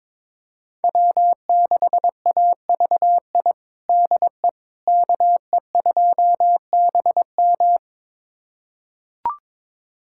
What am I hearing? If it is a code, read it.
Morse code, heard as a keyed tone, about 22 wpm, 700 Hz. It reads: W6AVI DE KE2BM
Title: USA Callsign DE Callsign - 5 Characters - 22wpm